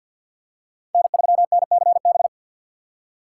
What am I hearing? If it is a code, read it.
Morse code, heard as a keyed tone, about 47 wpm, 700 Hz. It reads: N3NCB
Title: USA Call Signs - 5 Characters - 3x Character Spacing - 50wpm